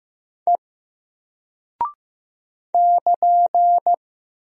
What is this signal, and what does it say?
Morse code, heard as a keyed tone, about 15 wpm, 700 Hz. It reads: E <KN>
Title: Single Character (Letter:Number:Punctuation:Prosign) Alt 15wpm